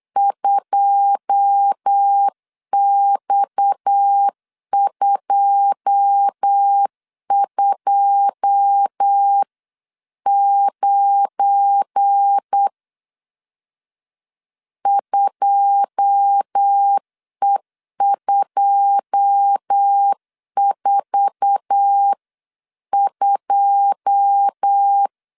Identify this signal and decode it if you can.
Morse code, heard as a keyed tone, about 8 wpm, 800 Hz. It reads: 2X229 2E242
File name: LEZIONI / 9 2 8wpm 3437